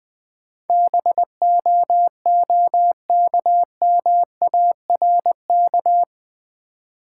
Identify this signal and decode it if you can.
Morse code, heard as a keyed tone, about 20 wpm, 700 Hz. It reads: BOOKMARK
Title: Easy Words with Hints - 20wpm